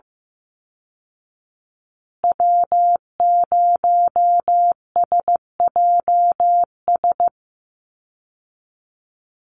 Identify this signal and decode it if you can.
Morse code, heard as a keyed tone, about 15 wpm, 700 Hz. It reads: W0SJS
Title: Call Signs Repeated 2x - 15wpm